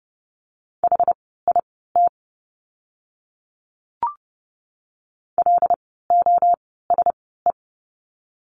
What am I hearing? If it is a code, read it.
Morse code, heard as a keyed tone, about 30 wpm, spaced as for 17 wpm, 700 Hz. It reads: HIT LOSE
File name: Lesson 17f - Letter u - Words 30wpm